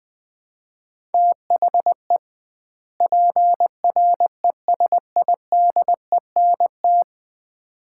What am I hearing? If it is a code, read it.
Morse code, heard as a keyed tone, about 21 wpm, 700 Hz. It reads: THE PRESIDENT
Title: 2x Word-Spacing - Sets of 2 Words - 20wpm